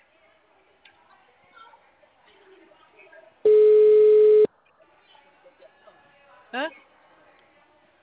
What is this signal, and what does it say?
Ringback tone (Europe, 425 Hz), with speech.